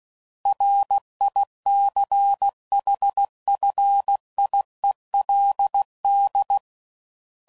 Morse code, heard as a keyed tone, about 16 wpm, 800 Hz. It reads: RICHFIELD
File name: CW Code Groups @15WPM / fromSM's / Cities